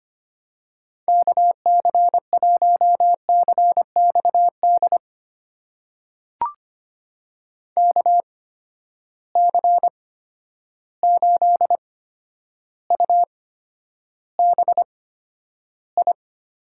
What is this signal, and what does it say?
Morse code, heard as a keyed tone, about 25 wpm, spaced as for 11 wpm, 700 Hz. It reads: KC1CXD KC8UBI
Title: USA Call Signs - 6 Characters - 8x Character Spacing - 25wpm